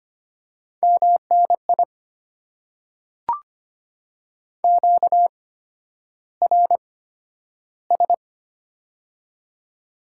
Morse code, heard as a keyed tone, about 25 wpm, spaced as for 11 wpm, 690 Hz. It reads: MNI QRS
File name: Top 90 QSO Elements - 8x Character Spacing - 25wpm